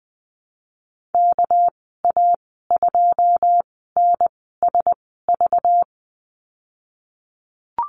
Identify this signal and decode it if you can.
Morse code, heard as a keyed tone, about 20 wpm, spaced as for 15 wpm, 700 Hz. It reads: KA2NSV
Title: 207 Lesson 33g - Letter j - Call Signs 20wpm